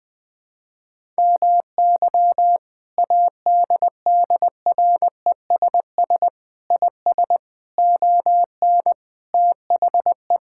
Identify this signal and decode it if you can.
Morse code, heard as a keyed tone, about 20 wpm, 700 Hz. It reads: MY ADDRESS IS ON THE
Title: Sentences from Top 1000 Words - Review Part 6 - 20wpm